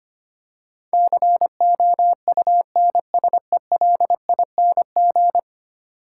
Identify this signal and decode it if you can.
Morse code, heard as a keyed tone, about 25 wpm, 690 Hz. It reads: COUNSELING
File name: 10 Letter Words - First 3 Letters Given - 25wpm